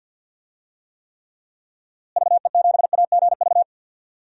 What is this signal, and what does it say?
Morse code, heard as a keyed tone, about 51 wpm, 700 Hz. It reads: VE7AGV
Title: DXCC-100 Call Signs - 6 Characters - Repeated 3x - 50wpm